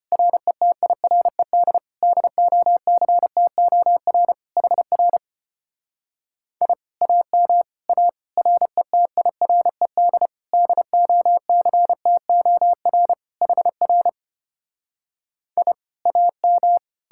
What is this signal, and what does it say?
Morse code, heard as a keyed tone, about 33 wpm, 700 Hz. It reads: RETIRED DOCTOR HR I AM A RETIRED DOCTOR HR I AM